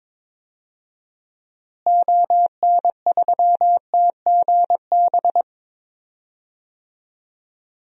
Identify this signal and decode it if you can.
Morse code, heard as a keyed tone, about 22 wpm, 700 Hz. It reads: ON3TGB
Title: DXCC-100 Call Signs - 6 Characters - Repeated 2x - 22wpm